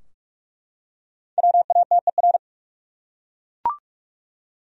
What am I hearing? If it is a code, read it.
Morse code, heard as a keyed tone, about 48 wpm, 700 Hz. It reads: WATER
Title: Top 100 Words Repeated 45wpm